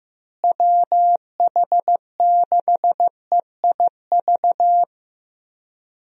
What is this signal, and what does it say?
Morse code, heard as a keyed tone, about 15 wpm, 700 Hz. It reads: WH6EIV
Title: USA Call Signs - 6 Characters - Repeated 3x - 15wpm